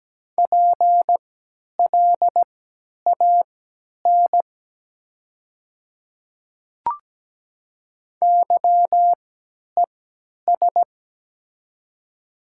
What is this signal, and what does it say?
Morse code, heard as a keyed tone, about 17 wpm, spaced as for 10 wpm, 690 Hz. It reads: PLAN YES
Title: 3x Character Spacing - Top 500 Words - 17wpm